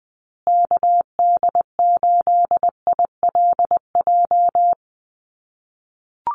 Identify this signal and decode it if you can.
Morse code, heard as a keyed tone, about 20 wpm, 700 Hz. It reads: KD8ILJ
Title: USA Call Signs Repeated 3x 20wpm